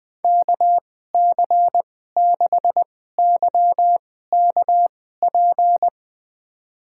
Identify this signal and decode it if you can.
Morse code, heard as a keyed tone, about 20 wpm, spaced as for 14 wpm, 700 Hz. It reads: KC6YKP
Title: USA Call Signs - 2x Character Spacing - 20wpm